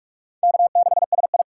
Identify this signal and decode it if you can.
Morse code, heard as a keyed tone, about 44 wpm, 700 Hz. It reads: K6SI
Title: USA Call Signs - 4 Characters - 3x Character Spacing - 45wpm